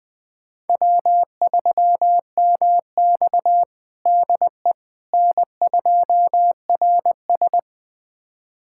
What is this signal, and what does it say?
Morse code, heard as a keyed tone, about 20 wpm, 700 Hz. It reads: W3MX DE N2RS
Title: USA Callsign DE Callsign - 4 Characters - 20wpm